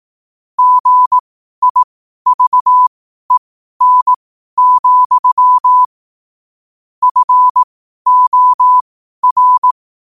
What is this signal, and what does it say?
Morse code, heard as a keyed tone, about 18 wpm, spaced as for 12 wpm, 1000 Hz. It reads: GIVEN, FOR
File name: qotdfiles / 12wpm